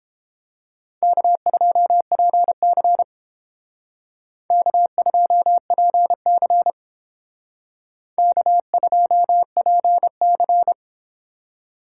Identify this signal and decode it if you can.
Morse code, heard as a keyed tone, about 30 wpm, 700 Hz. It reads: K2PC K2PC K2PC